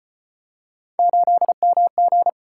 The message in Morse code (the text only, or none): 8MG